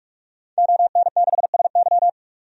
Morse code, heard as a keyed tone, about 46 wpm, 700 Hz. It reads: ON6SY
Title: DXCC-100 Call Signs - 5 Characters - 45wpm